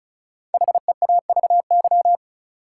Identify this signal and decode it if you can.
Morse code, heard as a keyed tone, about 36 wpm, 700 Hz. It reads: HEAVY
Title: Sentences from Top 1000 Words - Review Part 6 - New Word Drill - 35wpm